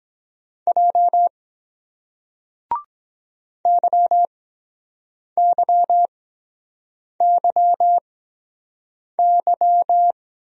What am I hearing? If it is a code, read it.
Morse code, heard as a keyed tone, about 23 wpm, spaced as for 7 wpm, 700 Hz. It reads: J YYYY